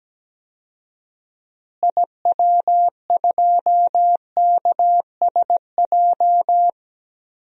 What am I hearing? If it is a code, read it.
Morse code, heard as a keyed tone, about 17 wpm, 700 Hz. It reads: IW2KSJ
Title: DXCC-100 Call Signs - 6 Characters - Repeated 2x - 17wpm